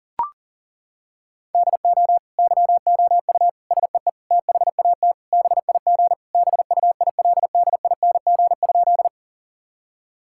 Morse code, heard as a keyed tone, about 37 wpm, 700 Hz. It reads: DO YOU SEE THAT BIG BUILDING?
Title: Sentences from Top 1000 Words - Review Part 5 - 40wpm